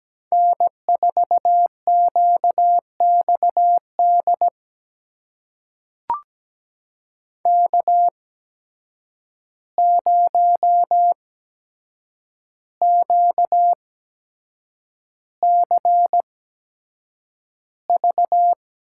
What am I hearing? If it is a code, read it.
Morse code, heard as a keyed tone, about 17 wpm, spaced as for 7 wpm, 700 Hz. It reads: N4QXD K0QCV
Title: USA Call Signs - 8x Character Spacing - 17wpm